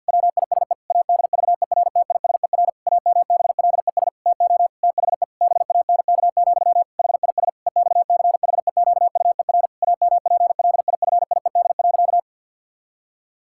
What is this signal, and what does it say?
Morse code, heard as a keyed tone, about 50 wpm, 700 Hz. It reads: WISE ADVERTISER AMBLES TO THE BANK, HIS EXCHEQUER AMPLIFIED.
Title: Pangram Challenge v2 - 50wpm